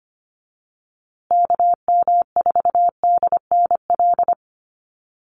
Morse code, heard as a keyed tone, about 25 wpm, 700 Hz. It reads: KM4DNL
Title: USA Call Signs Repeated 3x 25wpm